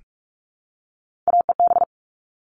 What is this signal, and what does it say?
Morse code, heard as a keyed tone, about 44 wpm, 700 Hz. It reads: AEB
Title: All Cut Numbers - 3-Digit 45wpm